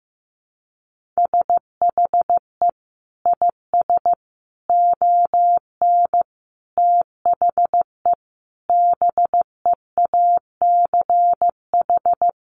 Morse code, heard as a keyed tone, about 15 wpm, 700 Hz. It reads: SHE IS ON THE BEACH
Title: Sentences from Top 1500 Words 15wpm